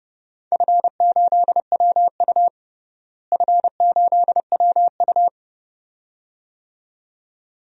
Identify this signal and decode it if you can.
Morse code, heard as a keyed tone, about 30 wpm, 700 Hz. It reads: F8WU F8WU
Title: DXCC-100 Call Signs - 4 Characters - Repeated 2x - 30wpm